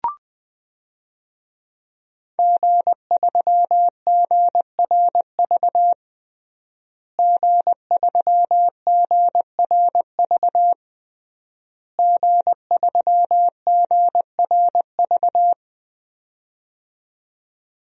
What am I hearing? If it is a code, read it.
Morse code, heard as a keyed tone, about 20 wpm, 700 Hz. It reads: G3GRV G3GRV G3GRV
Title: DXCC-100 Call Signs - 5 Characters - Repeated 3x - 20wpm